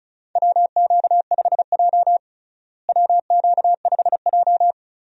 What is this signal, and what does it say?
Morse code, heard as a keyed tone, about 36 wpm, 700 Hz. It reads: WQ5J WQ5J
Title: USA Call Signs - 4 Characters - Repeated 2x - 35wpm